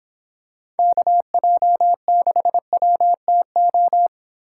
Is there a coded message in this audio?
Yes: Morse code from 0.8 to 4.1 s, heard as a keyed tone, about 26 wpm, 700 Hz. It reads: KJ6WTO